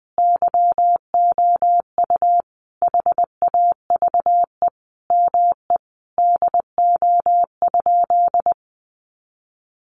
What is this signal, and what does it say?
Morse code, heard as a keyed tone, about 20 wpm, 700 Hz. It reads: YOU HAVE ME DO?
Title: Sentences from Top 200 Words 20wpm